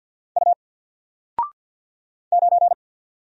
Morse code, heard as a keyed tone, about 48 wpm, 700 Hz. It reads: U 9
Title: Single Letter-Number Alt 50wpm